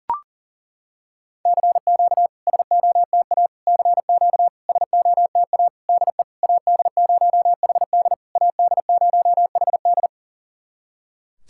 Morse code, heard as a keyed tone, about 40 wpm, 690 Hz. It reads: CQ SOTA CQ SOTA DE AD0HD AD0HD
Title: SOTA QSO Practice 40wpm